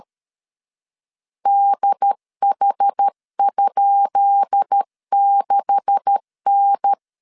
Morse code, heard as a keyed tone, about 13 wpm, 800 Hz. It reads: DH?6N
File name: LEZIONI / 10 SIGNS / 10B INT 12wpm 2414